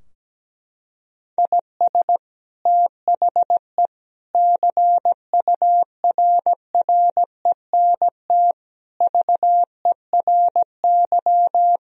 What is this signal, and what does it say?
Morse code, heard as a keyed tone, about 17 wpm, 700 Hz. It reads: IS THE CURRENT VERY
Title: Sentences from Top 1000 Words - Review Part 4 - 17wpm